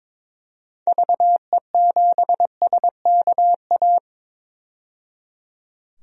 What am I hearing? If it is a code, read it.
Morse code, heard as a keyed tone, about 23 wpm, 700 Hz. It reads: VE7SKA
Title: DXCC-100 Call Signs - 6 Characters - 22wpm